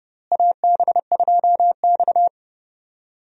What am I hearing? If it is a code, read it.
Morse code, heard as a keyed tone, about 30 wpm, 700 Hz. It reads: AB2X
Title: USA Call Signs - 4 Characters - Repeated 2x - 30wpm